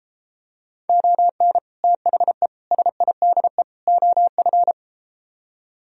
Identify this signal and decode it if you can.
Morse code, heard as a keyed tone, about 33 wpm, 700 Hz. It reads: ON THE SIDE OF